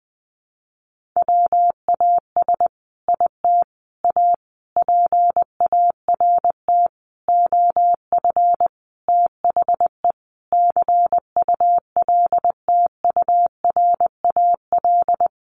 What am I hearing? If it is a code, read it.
Morse code, heard as a keyed tone, about 19 wpm, 700 Hz. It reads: WAS IT A PART OF THE CULTURAL
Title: Sentences from Top 1500 Words 20wpm